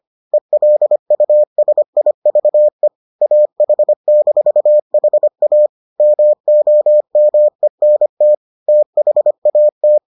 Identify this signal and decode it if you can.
Morse code, heard as a keyed tone, about 25 wpm, 600 Hz. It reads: ELUSIVE AH-HA MOMENT THAT